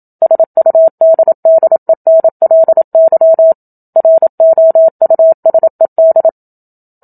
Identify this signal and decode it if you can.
Morse code, heard as a keyed tone, about 27 wpm, 650 Hz. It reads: SUDDENLY ROUSED
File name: Book Two / 01-Chapter 1 - 27 WPM